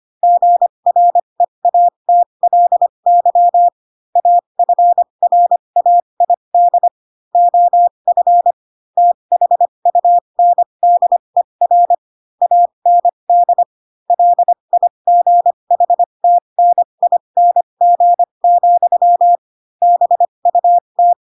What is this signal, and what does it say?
Morse code, heard as a keyed tone, about 25 wpm, spaced as for 22 wpm, 700 Hz. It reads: GREATLY AFRAID OF THUNDER AND LIGHTNING, BUT
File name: Pinocchio / Pinocchio ch6 18 25 ik1hge